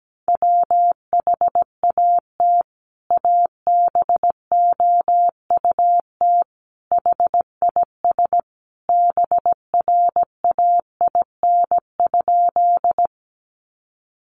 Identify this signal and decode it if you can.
Morse code, heard as a keyed tone, about 17 wpm, 700 Hz. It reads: WHAT ABOUT HIS BRAIN?
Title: Sentences from Top 1100 Words 17wpm